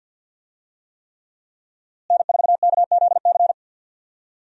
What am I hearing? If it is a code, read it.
Morse code, heard as a keyed tone, about 47 wpm, 700 Hz. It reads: N4KZC
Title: USA Call Signs - 5 Characters - Repeated 3x - 50wpm